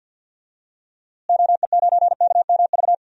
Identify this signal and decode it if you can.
Morse code, heard as a keyed tone, about 52 wpm, 700 Hz. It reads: OE9KMV
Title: DXCC-100 Call Signs - 6 Characters - Repeated 3x - 50wpm